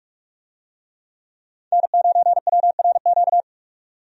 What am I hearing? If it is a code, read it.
Morse code, heard as a keyed tone, about 46 wpm, 700 Hz. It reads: N9WRQ
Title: USA Call Signs - 5 Characters - 45wpm